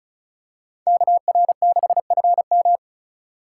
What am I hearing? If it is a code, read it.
Morse code, heard as a keyed tone, about 36 wpm, 700 Hz. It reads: KR6FM
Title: USA Call Signs - 4x Character Spacing - 35wpm